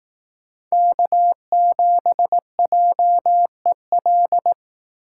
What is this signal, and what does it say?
Morse code, heard as a keyed tone, about 18 wpm, 700 Hz. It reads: K7JEL
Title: Speed Racing - USA Call Signs 15wpm